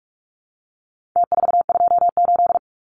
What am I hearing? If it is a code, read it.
Morse code, heard as a keyed tone, about 44 wpm, 700 Hz. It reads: T428
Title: Numbers - Contest Style - 4-Digit 45wpm